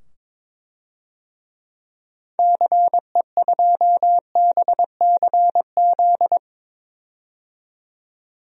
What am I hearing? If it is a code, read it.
Morse code, heard as a keyed tone, about 22 wpm, 700 Hz. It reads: CE2BCZ
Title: DXCC-100 Call Signs - 6 Characters - 22wpm